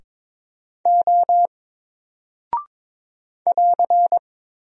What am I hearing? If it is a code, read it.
Morse code, heard as a keyed tone, about 22 wpm, 700 Hz. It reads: O <AR>
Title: Single Character (Letter:Number:Punctuation:Prosign) 22wpm